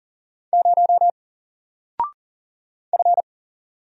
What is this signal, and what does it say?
Morse code, heard as a keyed tone, about 40 wpm, 700 Hz. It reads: 0 F
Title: Single Character (Letter:Number:Punctuation:Prosign) Alt 40wpm